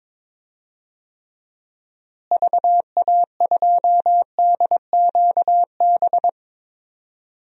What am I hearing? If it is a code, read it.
Morse code, heard as a keyed tone, about 21 wpm, 700 Hz. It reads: VA2DQB
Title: DXCC-100 Call Signs - 6 Characters - Repeated 3x - 22wpm